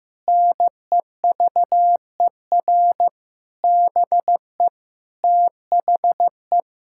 Morse code, heard as a keyed tone, about 15 wpm, 700 Hz. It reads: NEVER BE THE
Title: Sentences from Top 1600 Words 15wpm